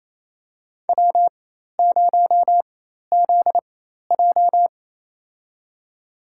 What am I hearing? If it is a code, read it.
Morse code, heard as a keyed tone, about 27 wpm, spaced as for 13 wpm, 700 Hz. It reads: W0ZJ
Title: USA Call Signs - 4 Characters - 4x Character Spacing - 28wpm